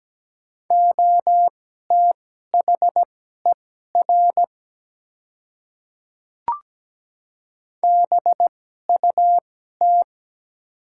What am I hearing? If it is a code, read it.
Morse code, heard as a keyed tone, about 17 wpm, spaced as for 12 wpm, 700 Hz. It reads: OTHER BUT
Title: 2x Character Spacing - Top 100 Words - 17wpm